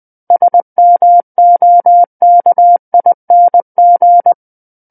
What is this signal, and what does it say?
Morse code, heard as a keyed tone, about 20 wpm, 700 Hz. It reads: SMOKING
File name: 20 WPM